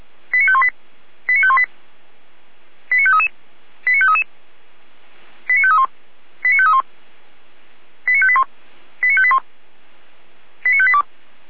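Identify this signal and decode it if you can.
Five-tone selective call (ZVEI), heard as quick runs of five tones, tones of about 70 ms. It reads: 89518 89518 89533 89533 89531 89531 89581 89581 89582
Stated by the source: ZVEI five-tone call 89505 received on an SDR, from the feuerwehr-orlamuende.org fire brigade audio archive.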